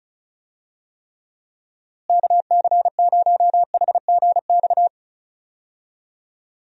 Morse code, heard as a keyed tone, about 36 wpm, 700 Hz. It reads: KC0HGX
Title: USA Call Signs - 6 Characters - Repeated 3x - 35wpm